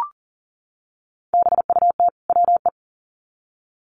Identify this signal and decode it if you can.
Morse code, heard as a keyed tone, about 40 wpm, 700 Hz. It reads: BUT WE
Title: Sets of 2 Words 40wpm